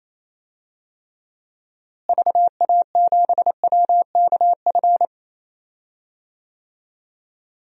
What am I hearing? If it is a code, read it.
Morse code, heard as a keyed tone, about 28 wpm, 700 Hz. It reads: VA7WKF
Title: DXCC-100 Call Signs - 6 Characters - Repeated 3x - 28wpm